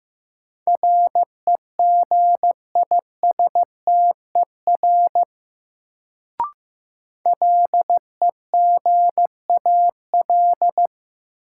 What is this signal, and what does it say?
Morse code, heard as a keyed tone, about 15 wpm, 700 Hz. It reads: REGISTER LEGAL
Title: Sentences from Top 1200 Words - New Word Drill Alt - 15wpm